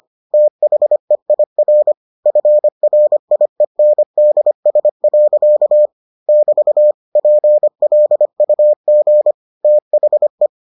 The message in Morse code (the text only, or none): THEIR FRIENDS. <BT> PLUG THE